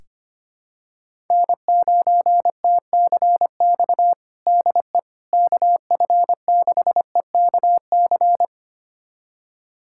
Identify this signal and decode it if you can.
Morse code, heard as a keyed tone, about 25 wpm, 700 Hz. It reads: N9TCX DE KF6EKC
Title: USA Callsign DE Callsign 25wpm